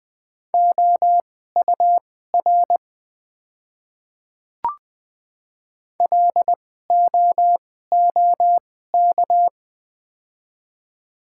Morse code, heard as a keyed tone, about 20 wpm, spaced as for 14 wpm, 700 Hz. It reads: OUR LOOK
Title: 2x Character Spacing - Top 200 Words - 20wpm